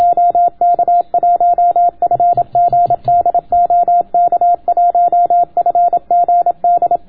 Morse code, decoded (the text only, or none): OK1FGD OK1FGD